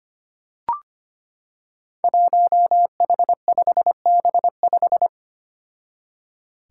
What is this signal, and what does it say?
Morse code, heard as a keyed tone, about 25 wpm, 700 Hz. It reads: 1H5B5
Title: Five Challenging Letter-Number Combinations 25wpm